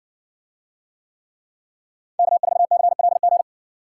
Morse code, heard as a keyed tone, about 55 wpm, 700 Hz. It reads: K4CLG